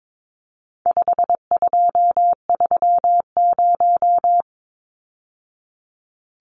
Morse code, heard as a keyed tone, about 22 wpm, 700 Hz. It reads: 5230
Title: Numbers - 4-Digit 22wpm